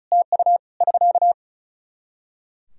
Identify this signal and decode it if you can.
Morse code, heard as a keyed tone, about 34 wpm, 700 Hz. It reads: TU <SK>
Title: Practice QSO - Exchange 4 - Ending - 35wpm